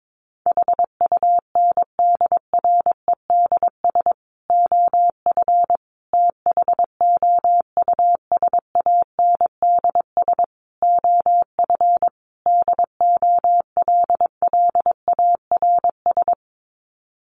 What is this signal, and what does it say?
Morse code, heard as a keyed tone, about 22 wpm, 700 Hz. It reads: HUNDREDS OF THOUSANDS OF DOLLARS
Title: Sets of 5 Words Encore 22wpm